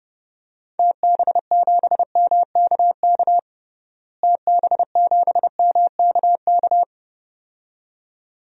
Morse code, heard as a keyed tone, about 32 wpm, 700 Hz. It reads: TB7MKK TB7MKK
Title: DXCC-100 Call Signs - 6 Characters - Repeated 2x - 30wpm